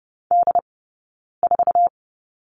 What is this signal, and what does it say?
Morse code, heard as a keyed tone, about 30 wpm, spaced as for 9 wpm, 690 Hz. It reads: D4
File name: course / 30:10wpm / 086 Lesson 15d - Number 5 - Two Characters 30wpm